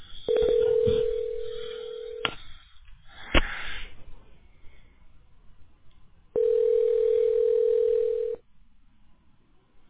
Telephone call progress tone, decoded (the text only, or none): ringback tone